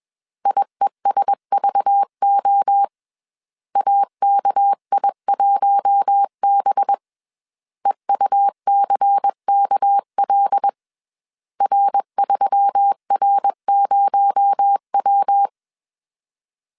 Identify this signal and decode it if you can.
Morse code, heard as a keyed tone, about 21 wpm, 800 Hz. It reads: IESVO AKI1B EUCKL R3R0W
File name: CW / DET / 1 6 1500C S B 20wpm 2048